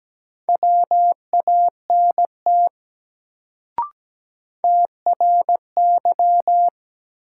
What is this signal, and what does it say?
Morse code, heard as a keyed tone, about 17 wpm, 700 Hz. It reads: WANT TRY
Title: Sentences from Top 200 Words - New Word Drill - 17wpm